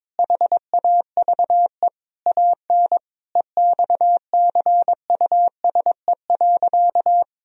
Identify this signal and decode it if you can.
Morse code, heard as a keyed tone, about 21 wpm, 700 Hz. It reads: HAVE AN EXCUSE.
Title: Sentences from Top 1000 Words - Review Part 4 - 22wpm